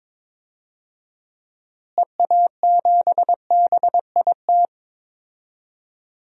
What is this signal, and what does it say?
Morse code, heard as a keyed tone, about 22 wpm, 700 Hz. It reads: EA7BIT
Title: DXCC-100 Call Signs - 6 Characters - Repeated 2x - 22wpm